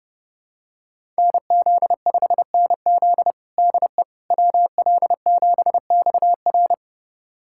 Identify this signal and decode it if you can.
Morse code, heard as a keyed tone, about 30 wpm, 700 Hz. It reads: NZ5NZ DE WL7XR
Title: USA Callsign DE Callsign - 5 Characters - 30wpm